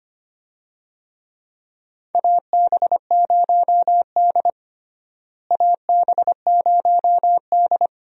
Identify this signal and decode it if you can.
Morse code, heard as a keyed tone, about 25 wpm, 700 Hz. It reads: AB0D AB0D
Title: USA Call Signs - 4 Characters - Repeated 2x - 25wpm